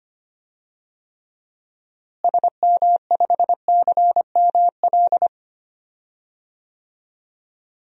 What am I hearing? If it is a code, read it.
Morse code, heard as a keyed tone, about 25 wpm, 700 Hz. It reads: SM5CML
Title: DXCC-100 Call Signs - 6 Characters - Repeated 3x - 25wpm